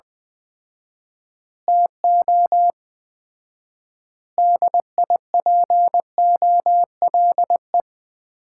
Morse code, heard as a keyed tone, about 20 wpm, 700 Hz. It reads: TO DIPOLE
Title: Common QSO Phrases - 4x Word-Spacing - 20wpm